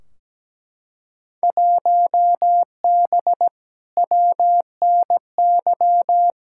Morse code, heard as a keyed tone, about 17 wpm, 700 Hz. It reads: 1B WNY
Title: Field Day Exchanges - 17wpm